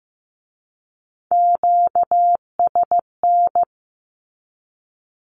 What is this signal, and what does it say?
Morse code, heard as a keyed tone, about 15 wpm, 700 Hz. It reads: QSN
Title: Q-Codes with Spoken Meaning - 15wpm